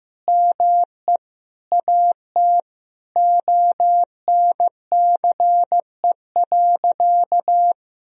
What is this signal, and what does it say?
Morse code, heard as a keyed tone, about 16 wpm, 700 Hz. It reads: ME AT ONCE.
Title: Sentences from Top 500 Words and Abbreviations 15wpm